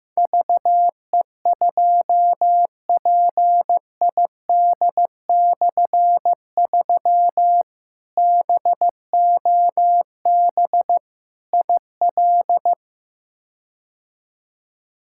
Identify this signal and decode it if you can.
Morse code, heard as a keyed tone, about 15 wpm, 700 Hz. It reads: VE2PID/3 BOB IL
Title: K1USN SST QSO Practice - 15wpm